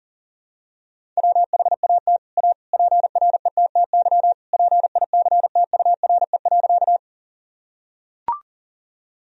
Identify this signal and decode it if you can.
Morse code, heard as a keyed tone, about 40 wpm, 700 Hz. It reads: WHAT A PRETTY PICTURE.
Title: Sentences from Top 1000 Words - Review Part 1 - 40wpm